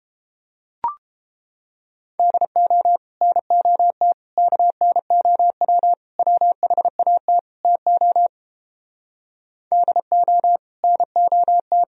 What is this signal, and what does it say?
Morse code, heard as a keyed tone, about 29 wpm, 700 Hz. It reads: DO NOT KNOW WHAT TO DO NOT